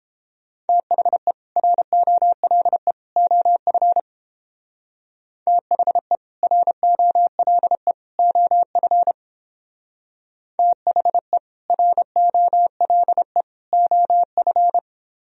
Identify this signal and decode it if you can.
Morse code, heard as a keyed tone, about 31 wpm, 700 Hz. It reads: THE ROLE OF THE ROLE OF THE ROLE OF